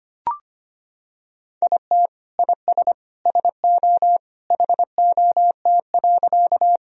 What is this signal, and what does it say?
Morse code, heard as a keyed tone, about 25 wpm, 700 Hz. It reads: IT IS SO HOT.
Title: Sentences from Top 1000 Words - Review Part 3 - 25wpm